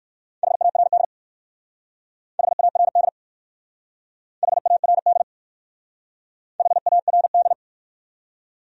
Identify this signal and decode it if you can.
Morse code, heard as a keyed tone, about 56 wpm, 700 Hz. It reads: HARD HARD HARD HARD